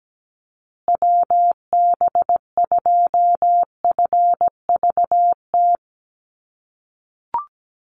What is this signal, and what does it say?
Morse code, heard as a keyed tone, about 17 wpm, 700 Hz. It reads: WB2FVT
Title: USA Call Signs Repeated 3x 17wpm